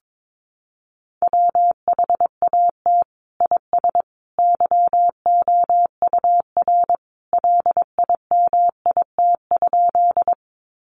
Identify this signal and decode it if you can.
Morse code, heard as a keyed tone, about 22 wpm, 700 Hz. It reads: WHAT IS YOUR LIMIT?
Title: Sentences from Top 1500 Words 22wpm